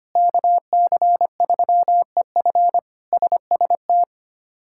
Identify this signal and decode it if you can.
Morse code, heard as a keyed tone, about 25 wpm, 700 Hz. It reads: KC3EF SST
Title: K1USN SST QSO Practice - 25wpm